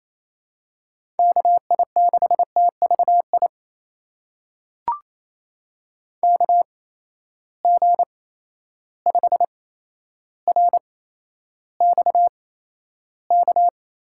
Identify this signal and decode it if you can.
Morse code, heard as a keyed tone, about 28 wpm, spaced as for 12 wpm, 700 Hz. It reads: KI6TVI KG5RXK
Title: USA Call Signs - 6 Characters - 8x Character Spacing - 28wpm